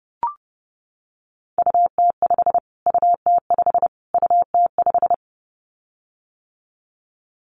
Morse code, heard as a keyed tone, about 30 wpm, 700 Hz. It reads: UT5 UT5 UT5
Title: Callsign Prefixes for Top 25 DXCC - Repeated 3x - 30wpm